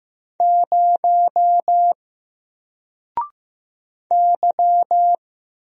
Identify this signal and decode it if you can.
Morse code, heard as a keyed tone, about 15 wpm, 700 Hz. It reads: 0 Y
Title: Single Letter-Number Alt 15wpm